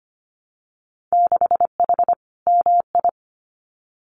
Morse code, heard as a keyed tone, about 25 wpm, 700 Hz. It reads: 6H MI
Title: Winter Field Day Exchanges Encore 25wpm